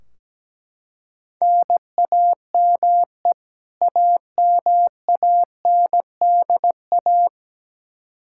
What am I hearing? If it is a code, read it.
Morse code, heard as a keyed tone, about 17 wpm, 700 Hz. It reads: NAME AMANDA
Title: Practice QSO - Exchange 1 - Beginning - 17wpm